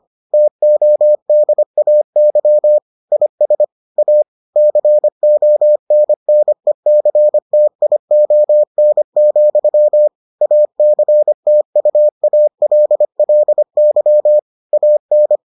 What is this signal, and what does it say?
Morse code, heard as a keyed tone, about 25 wpm, 600 Hz. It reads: TODAY IS A CONNECTION, ACTUALLY AN